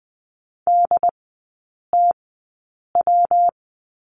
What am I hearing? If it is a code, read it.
Morse code, heard as a keyed tone, about 20 wpm, spaced as for 8 wpm, 700 Hz. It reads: DTW
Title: 127 Lesson 21e - Number 6 - Three Characters 20wpm